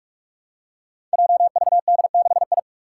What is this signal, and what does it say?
Morse code, heard as a keyed tone, about 46 wpm, 700 Hz. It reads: JVD6I
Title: Five Challenging Letter-Number Combinations 45wpm